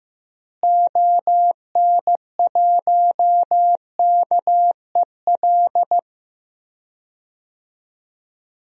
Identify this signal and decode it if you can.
Morse code, heard as a keyed tone, about 15 wpm, 700 Hz. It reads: ON1KEL
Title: DXCC-100 Call Signs - 6 Characters - Repeated 3x - 15wpm